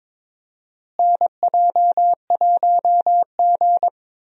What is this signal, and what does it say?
Morse code, heard as a keyed tone, about 22 wpm, 700 Hz. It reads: NJ1G
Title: USA Call Signs - 4 Characters - Repeated 2x - 22wpm